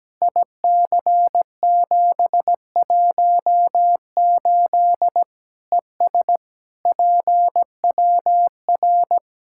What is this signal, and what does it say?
Morse code, heard as a keyed tone, about 17 wpm, 700 Hz. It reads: IC718 ES PWR
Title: Practice QSO - Exchange 2 - Info - 17wpm